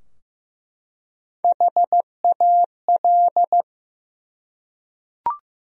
Morse code, heard as a keyed tone, about 15 wpm, 700 Hz. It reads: HAL
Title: USA Common Names - Max 5 Letters - 15wpm